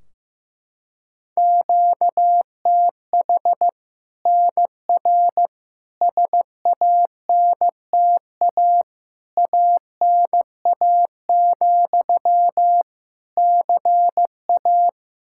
Morse code, heard as a keyed tone, about 15 wpm, 700 Hz. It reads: QTH NR SANTA ANA, CA
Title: Practice QSO - Exchange 1 - Beginning - 15wpm